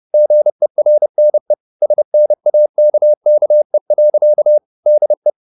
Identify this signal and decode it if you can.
Morse code, heard as a keyed tone, about 30 wpm, 600 Hz. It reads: GERNE SNAKKE. DE
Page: Ferie 30 wpm